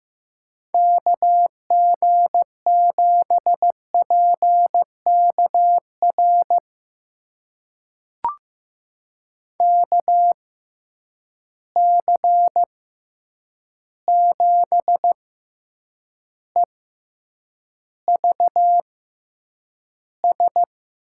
Morse code, heard as a keyed tone, about 15 wpm, spaced as for 8 wpm, 700 Hz. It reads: KG7PKR KC7EVS
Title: USA Call Signs - 6 Characters - 6x Character Spacing - 15wpm